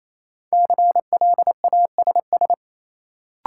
Morse code, heard as a keyed tone, about 28 wpm, 700 Hz. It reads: CLASS
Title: Sentences from Top 1000 Words - Review Part 2 - New Word Drill - 28wpm